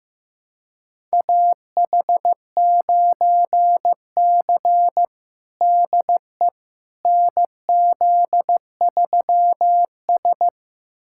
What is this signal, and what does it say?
Morse code, heard as a keyed tone, about 15 wpm, 700 Hz. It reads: AH9C DE NZ3S
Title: USA Callsign DE Callsign - 4 Characters - 15wpm